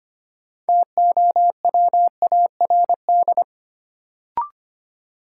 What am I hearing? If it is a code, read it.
Morse code, heard as a keyed tone, about 25 wpm, 700 Hz. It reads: TOWARD
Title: Sentences from Top 1300 Words - New Word Drill Alt - 25wpm